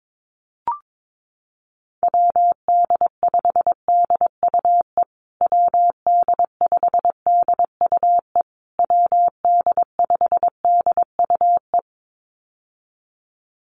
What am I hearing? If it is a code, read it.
Morse code, heard as a keyed tone, about 22 wpm, 700 Hz. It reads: WD5DUE WD5DUE WD5DUE
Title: USA Call Signs Repeated 3x 22wpm